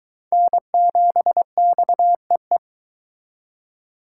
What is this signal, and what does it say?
Morse code, heard as a keyed tone, about 22 wpm, 700 Hz. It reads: N7XEE